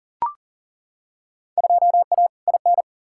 Morse code, heard as a keyed tone, about 37 wpm, 700 Hz. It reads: 2A IN
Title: Field Day Exchanges - 40wpm